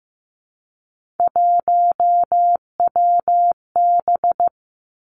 Morse code, heard as a keyed tone, about 15 wpm, 700 Hz. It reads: 1WB
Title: Three Challenging Letter-Number Combinations 15wpm